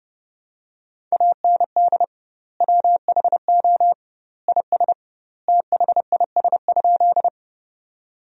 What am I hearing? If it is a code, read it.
Morse code, heard as a keyed tone, about 30 wpm, 700 Hz. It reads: AND WHO IS THIS?
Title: 2x Word-Spacing - Sentences from Top 100 Words - 30wpm